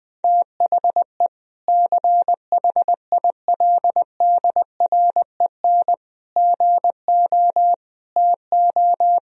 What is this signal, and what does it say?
Morse code, heard as a keyed tone, about 21 wpm, 700 Hz. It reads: THE CHILDREN GO TO
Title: Sentences from Top 1000 Words - Review Part 1 - 20wpm